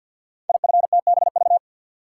Morse code, heard as a keyed tone, about 50 wpm, 700 Hz. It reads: IFTBV
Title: Code Groups - 5 Letters - 50wpm